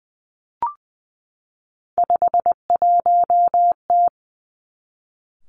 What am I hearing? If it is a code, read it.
Morse code, heard as a keyed tone, about 20 wpm, 700 Hz. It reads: 51T
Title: Three Challenging Letter-Number Combinations 20wpm